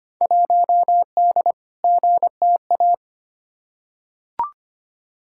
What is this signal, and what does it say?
Morse code, heard as a keyed tone, about 25 wpm, 700 Hz. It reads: 1D GTA
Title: Field Day Exchanges - 25wpm